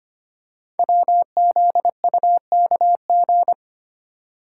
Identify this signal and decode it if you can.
Morse code, heard as a keyed tone, about 25 wpm, 700 Hz. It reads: WZUKG
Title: Code Groups - 5 Letters - 25wpm